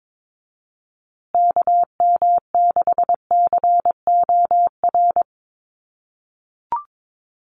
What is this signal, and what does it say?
Morse code, heard as a keyed tone, about 22 wpm, 700 Hz. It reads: KM6COR
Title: USA Call Signs Repeated 3x 22wpm